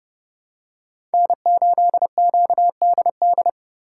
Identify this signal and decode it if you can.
Morse code, heard as a keyed tone, about 30 wpm, 700 Hz. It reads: N8QDD
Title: USA Call Signs - 2x Character Spacing - 30wpm